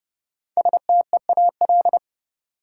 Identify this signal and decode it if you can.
Morse code, heard as a keyed tone, about 30 wpm, 700 Hz. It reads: STEAL
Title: Five Letter Words - 30wpm